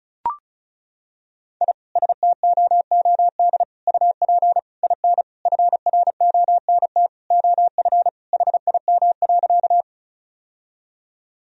Morse code, heard as a keyed tone, about 34 wpm, 700 Hz. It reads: I STOOD UP IN FRONT OF HIM.
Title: Sentences from Top 1000 Words - Review Part 2 - 35wpm